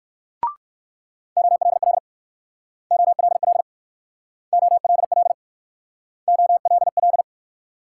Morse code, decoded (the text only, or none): OLL OLL OLL OLL